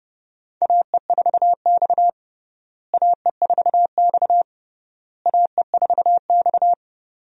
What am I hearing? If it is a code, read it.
Morse code, heard as a keyed tone, about 29 wpm, 700 Hz. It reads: AE4X AE4X AE4X
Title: USA Call Signs - 4 Characters - Repeated 3x - 30wpm